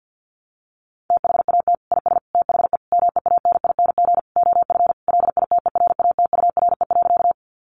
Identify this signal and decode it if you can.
Morse code, heard as a keyed tone, about 50 wpm, 700 Hz. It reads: THAT IS THE MEANING OF LITERATURE.
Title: Sentences from Top 900 Words 50wpm